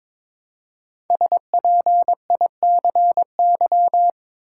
Morse code, heard as a keyed tone, about 22 wpm, 700 Hz. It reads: SPICY
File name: 5 Letter Words - First 3 Letters Given - 22wpm